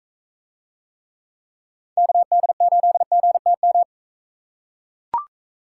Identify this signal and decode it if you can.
Morse code, heard as a keyed tone, about 40 wpm, 700 Hz. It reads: KD8GTM